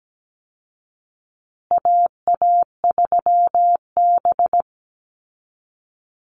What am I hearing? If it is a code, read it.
Morse code, heard as a keyed tone, about 17 wpm, 700 Hz. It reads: AA3B
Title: Call Signs Repeated 2x - 17wpm